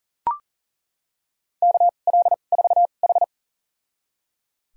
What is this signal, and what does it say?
Morse code, heard as a keyed tone, about 40 wpm, spaced as for 29 wpm, 700 Hz. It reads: KL4H
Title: USA Call Signs - 4 Characters - 2x Character Spacing - 40wpm